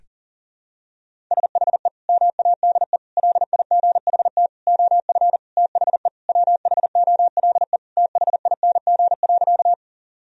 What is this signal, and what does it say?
Morse code, heard as a keyed tone, about 37 wpm, 700 Hz. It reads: SHE MADE LIGHT OF THE WHOLE THING.
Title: Sentences from Top 1000 Words - Review Part 1 - 40wpm